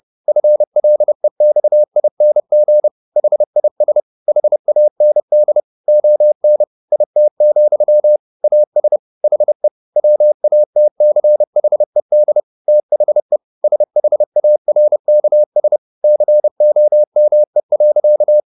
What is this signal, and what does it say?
Morse code, heard as a keyed tone, about 30 wpm, 600 Hz. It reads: FLEXING HIS HAND ON IT, AS HE WATCHED THE SHARKS COME.